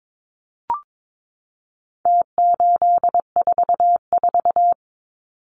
Morse code, heard as a keyed tone, about 22 wpm, 700 Hz. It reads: T844
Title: Numbers - Contest Style - 4-Digit 22wpm